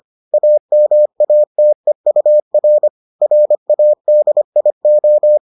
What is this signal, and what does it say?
Morse code, heard as a keyed tone, about 25 wpm, 600 Hz. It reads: AMATEUR RADIO